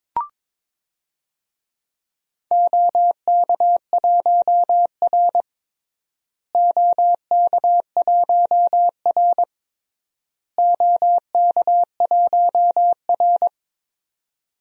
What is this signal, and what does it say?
Morse code, heard as a keyed tone, about 21 wpm, 700 Hz. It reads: OK1R OK1R OK1R
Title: DXCC-100 Call Signs - 4 Characters - Repeated 3x - 22wpm